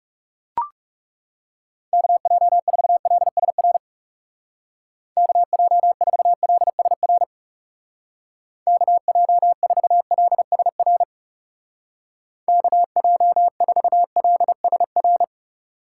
Morse code, heard as a keyed tone, about 35 wpm, 700 Hz. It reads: KJ4LSR KJ4LSR KJ4LSR KJ4LSR
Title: Speed Racing - USA 2x3 Call Signs 30wpm